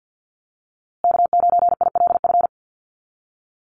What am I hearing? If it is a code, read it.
Morse code, heard as a keyed tone, about 50 wpm, 700 Hz. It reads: K9ILF
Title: USA Call Signs Repeated 3x 50wpm